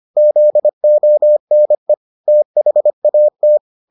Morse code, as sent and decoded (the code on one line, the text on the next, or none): --.. --- -. / - .... .- -
ZONE THAT